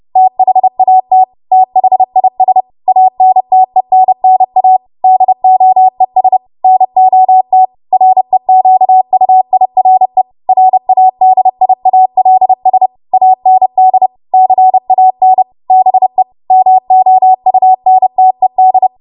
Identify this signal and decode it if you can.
Morse code, heard as a keyed tone, about 30 wpm, 750 Hz. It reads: THAT THIS ANTENNA DOES NOT REQUIRE RADIALS AND CAN BE MOUNTED
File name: Morse / 30WPM